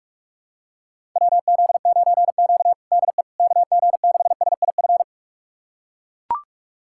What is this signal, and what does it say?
Morse code, heard as a keyed tone, about 42 wpm, 700 Hz. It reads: WZ9Q DE KG6SIF
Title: USA Callsign DE Callsign 45wpm